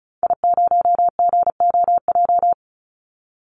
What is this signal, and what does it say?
Morse code, heard as a keyed tone, about 34 wpm, 700 Hz. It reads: I0GOJ
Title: Call Signs - 35wpm